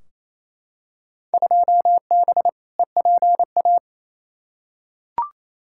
Morse code, heard as a keyed tone, about 28 wpm, 700 Hz. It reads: 2B EPA
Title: Field Day Exchanges Encore - 28wpm